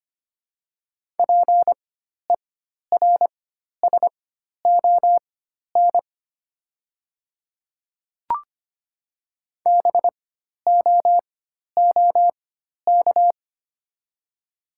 Morse code, heard as a keyed tone, about 25 wpm, spaced as for 12 wpm, 690 Hz. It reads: PERSON BOOK
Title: 4x Character Spacing - Top 500 Words - 25wpm